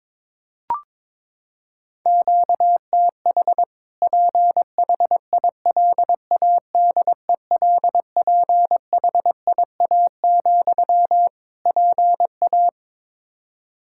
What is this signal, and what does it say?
Morse code, heard as a keyed tone, about 22 wpm, 700 Hz. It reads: QTH PHILADELPHIA, PA
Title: Practice QSO - Exchange 1 - Beginning - 22wpm